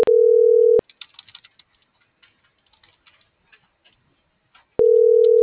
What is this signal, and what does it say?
Ringback tone (North America), 440 + 480 Hz.